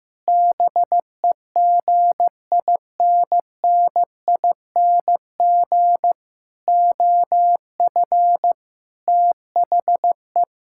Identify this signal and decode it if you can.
Morse code, heard as a keyed tone, about 16 wpm, 700 Hz. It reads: BEGINNING OF THE
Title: Sentences from Top 1000 Words - Review Part 5 - 15wpm